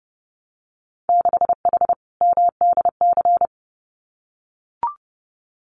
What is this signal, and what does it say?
Morse code, heard as a keyed tone, about 30 wpm, 700 Hz. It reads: 6H MDC
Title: Winter Field Day Exchanges Encore 30wpm